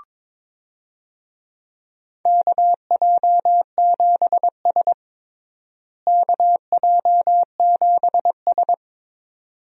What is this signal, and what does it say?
Morse code, heard as a keyed tone, about 22 wpm, 700 Hz. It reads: KJ7S KJ7S
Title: USA Call Signs - 4 Characters - Repeated 2x - 22wpm